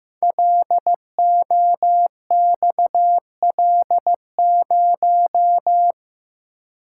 Morse code, heard as a keyed tone, about 15 wpm, 700 Hz. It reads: LOXL0
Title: Code Groups - 5 Letter:Numbers - 15wpm